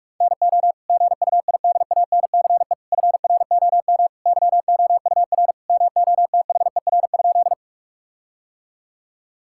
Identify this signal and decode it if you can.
Morse code, heard as a keyed tone, about 46 wpm, 700 Hz. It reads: NO GUIDANCE FROM YOUR MOTHER?
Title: Sentences from Top 1600 Words 45wpm